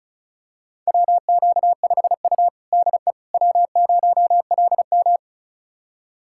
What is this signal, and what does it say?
Morse code, heard as a keyed tone, about 36 wpm, 700 Hz. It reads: WQ5U DE W0LM
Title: USA Callsign DE Callsign - 4 Characters - 35wpm